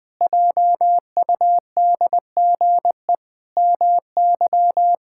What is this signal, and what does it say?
Morse code, heard as a keyed tone, about 21 wpm, 700 Hz. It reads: JUDGE MY
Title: Pangram Challenge v2 - 20wpm